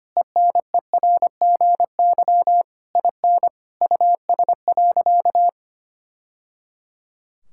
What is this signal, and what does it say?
Morse code, heard as a keyed tone, about 25 wpm, 700 Hz. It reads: ENERGY IN US.
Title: Sentences from Top 1000 Words - Review Part 6 - 25wpm